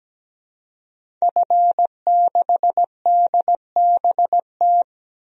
Morse code, heard as a keyed tone, about 17 wpm, 700 Hz. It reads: F6DBT
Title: DXCC-100 Call Signs - 5 Characters - 17wpm